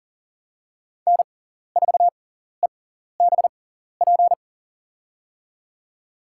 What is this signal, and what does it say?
Morse code, heard as a keyed tone, about 38 wpm, spaced as for 14 wpm, 690 Hz. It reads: N4EBP
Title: USA Call Signs - 5 Characters - 6x Character Spacing - 40wpm